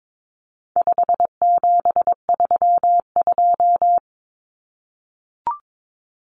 Morse code, heard as a keyed tone, about 22 wpm, 700 Hz. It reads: 5732
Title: Numbers - 4-Digit 22wpm